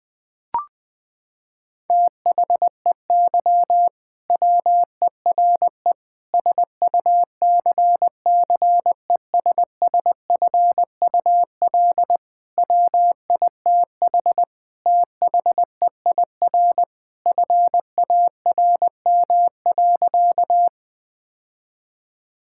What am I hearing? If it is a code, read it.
Morse code, heard as a keyed tone, about 19 wpm, 700 Hz. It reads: THEY WERE SUCCESSFUL WITH THEIR FARM.
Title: Sentences from Top 1000 Words - Review Part 6 - 20wpm